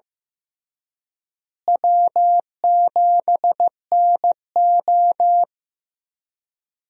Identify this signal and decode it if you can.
Morse code, heard as a keyed tone, about 15 wpm, 700 Hz. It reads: W7NO
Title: USA Call Signs - 4 Characters - Repeated 2x - 15wpm